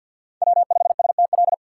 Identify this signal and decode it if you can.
Morse code, heard as a keyed tone, about 52 wpm, 700 Hz. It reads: W5STL
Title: USA Call Signs - 5 Characters - 4x Character Spacing - 50wpm